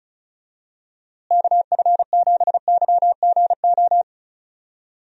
Morse code, heard as a keyed tone, about 36 wpm, 700 Hz. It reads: KF7YGO
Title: USA Call Signs - 6 Characters - Repeated 2x - 35wpm